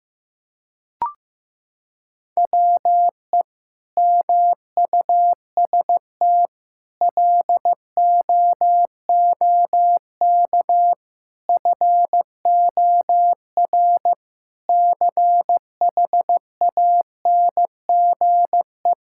Morse code, heard as a keyed tone, about 16 wpm, 700 Hz. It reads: WE MUST LOOK FOR CHANGE
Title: Sentences from Top 1600 Words 15wpm